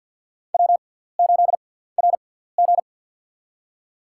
Morse code, heard as a keyed tone, about 51 wpm, spaced as for 17 wpm, 700 Hz. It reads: W8RG
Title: USA Call Signs - 4 Characters - 6x Character Spacing - 50wpm